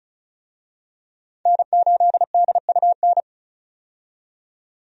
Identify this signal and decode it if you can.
Morse code, heard as a keyed tone, about 34 wpm, 700 Hz. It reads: N8DUN
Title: USA Call Signs - 5 Characters - 35wpm